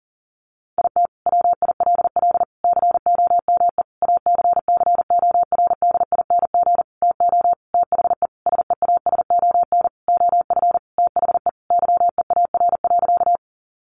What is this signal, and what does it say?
Morse code, heard as a keyed tone, about 40 wpm, 700 Hz. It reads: IT WILL COME ACCORDING TO THE SEASON OF THE YEAR.
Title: Sentences from Top 1400 Words 40wpm